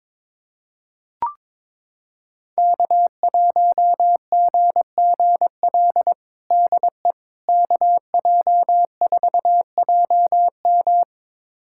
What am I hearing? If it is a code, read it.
Morse code, heard as a keyed tone, about 21 wpm, 700 Hz. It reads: K1GGL DE KJ4JM
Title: USA Callsign DE Callsign - 5 Characters - 22wpm